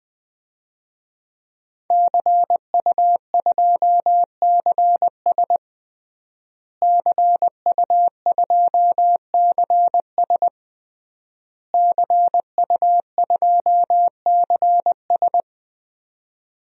Morse code, heard as a keyed tone, about 21 wpm, 700 Hz. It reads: CU2CS CU2CS CU2CS
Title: DXCC-100 Call Signs - 5 Characters - Repeated 3x - 20wpm